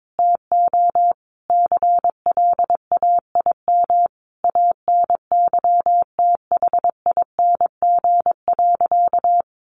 Morse code, heard as a keyed tone, about 22 wpm, 700 Hz. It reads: TO CLAIM ANYTHING.
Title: Sentences from Top 1500 Words 22wpm